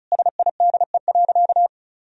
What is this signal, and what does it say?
Morse code, heard as a keyed tone, about 36 wpm, 700 Hz. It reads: SIDE.
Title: Sentences from Top 1000 Words - Review Part 1 - 35wpm